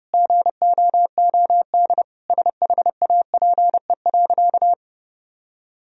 Morse code, heard as a keyed tone, about 30 wpm, 700 Hz. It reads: GOOD SHAPE.
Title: Sentences from Top 1000 Words - Review Part 3 - 30wpm